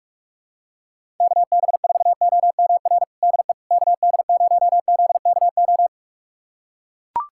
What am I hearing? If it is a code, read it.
Morse code, heard as a keyed tone, about 41 wpm, 700 Hz. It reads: KB4OMR DE KD0ZKO
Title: USA Callsign DE Callsign - 6 Characters - 45wpm